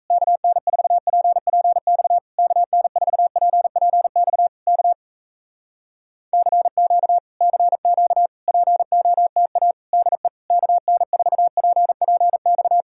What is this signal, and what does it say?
Morse code, heard as a keyed tone, about 38 wpm, 700 Hz. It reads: KN4PPX KN4PPX K CQ CQ POTA DE KN4PPX